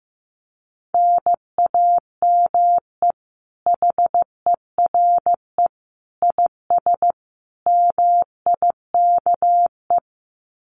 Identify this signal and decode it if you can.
Morse code, heard as a keyed tone, about 15 wpm, 700 Hz. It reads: NAME HERE IS MIKE
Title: Common QSO Phrases 15wpm